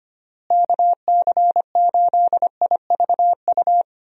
Morse code, heard as a keyed tone, about 25 wpm, 700 Hz. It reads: KC8IVU
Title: USA Call Signs - 6 Characters - Repeated 3x - 25wpm